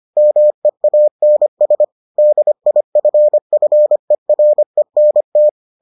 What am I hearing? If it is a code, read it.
Morse code, heard as a keyed tone, about 25 wpm, 600 Hz. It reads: MEANS DIFFERENT